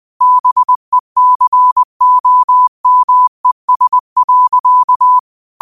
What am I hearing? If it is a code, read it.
Morse code, heard as a keyed tone, about 20 wpm, 1000 Hz. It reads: BECOMES.
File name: qotdfiles / 20wpm